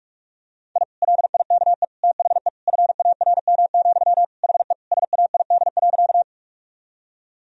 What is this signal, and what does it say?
Morse code, heard as a keyed tone, about 41 wpm, 700 Hz. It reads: I LIKE THE FARM, HE SAID.
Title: Sentences from Top 1000 Words - Review Part 2 - 45wpm